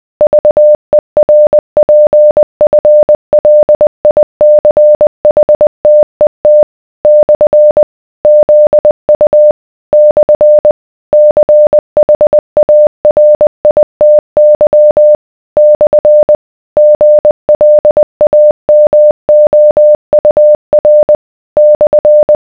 Morse code, heard as a keyed tone, about 20 wpm, 600 Hz. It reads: VERPFLICHTET / ZU / CHARITY / GLAMOUR /